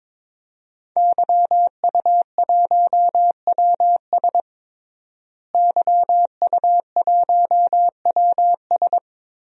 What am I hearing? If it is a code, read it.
Morse code, heard as a keyed tone, about 22 wpm, 700 Hz. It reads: YU1WS YU1WS
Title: DXCC-100 Call Signs - 5 Characters - Repeated 2x - 22wpm